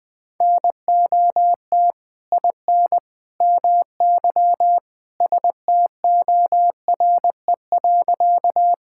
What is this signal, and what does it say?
Morse code, heard as a keyed tone, about 19 wpm, 700 Hz. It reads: NOT IN MY STORE.
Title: Sentences from Top 1000 Words - Review Part 5 - 20wpm